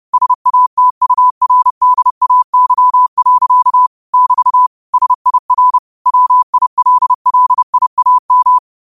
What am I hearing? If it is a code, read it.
Morse code, heard as a keyed tone, about 30 wpm, 1000 Hz. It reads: SATURDAY. <BT> SIR WILLIAM